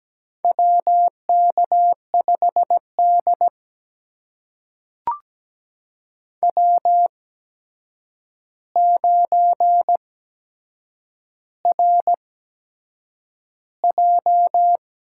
Morse code, heard as a keyed tone, about 17 wpm, spaced as for 7 wpm, 700 Hz. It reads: WK5D W9RJ
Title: USA Call Signs - 4 Characters - 8x Character Spacing - 17wpm